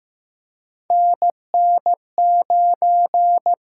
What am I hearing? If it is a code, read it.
Morse code, heard as a keyed tone, about 15 wpm, 700 Hz. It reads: NN9
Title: USA Call Signs - 6x Character Spacing - 15wpm